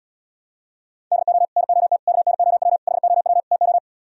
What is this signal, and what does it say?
Morse code, heard as a keyed tone, about 77 wpm, 700 Hz. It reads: DO NOT KNOW HOW TO